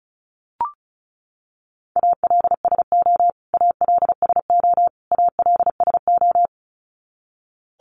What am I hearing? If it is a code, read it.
Morse code, heard as a keyed tone, about 34 wpm, 700 Hz. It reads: ALSO ALSO ALSO
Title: Top 500 Words Repeated 35wpm